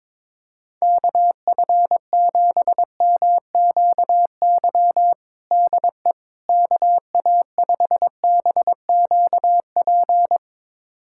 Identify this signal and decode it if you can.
Morse code, heard as a keyed tone, about 21 wpm, 700 Hz. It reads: KF7MQY DE KA5BQP
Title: USA Callsign DE Callsign - 6 Characters - 22wpm